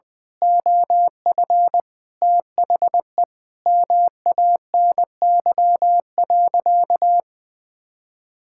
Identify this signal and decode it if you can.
Morse code, heard as a keyed tone, about 20 wpm, 700 Hz. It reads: OF THE MANY.
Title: Sentences from Top 1000 Words - Review Part 6 - 20wpm